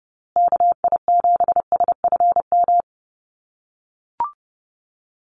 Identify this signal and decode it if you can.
Morse code, heard as a keyed tone, about 30 wpm, 700 Hz. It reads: KI7SFM
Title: USA Call Signs Repeated 3x 30wpm